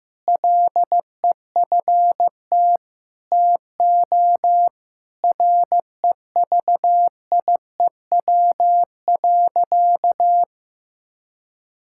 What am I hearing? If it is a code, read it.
Morse code, heard as a keyed tone, about 15 wpm, 700 Hz. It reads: LEFT TO REVIEW.
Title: Sentences from Top 1000 Words - Review Part 6 - 15wpm